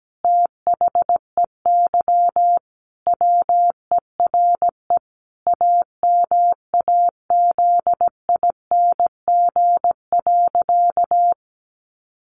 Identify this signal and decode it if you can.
Morse code, heard as a keyed tone, about 17 wpm, 700 Hz. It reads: THEY WERE AMAZING.
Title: Sentences from Top 1500 Words 17wpm